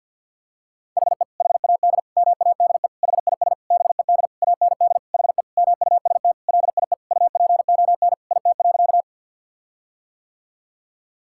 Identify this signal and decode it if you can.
Morse code, heard as a keyed tone, about 50 wpm, 700 Hz. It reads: HE HAD MADE HIS BED AND HE MUST LIE UPON IT.
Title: Sentences from Top 1600 Words 50wpm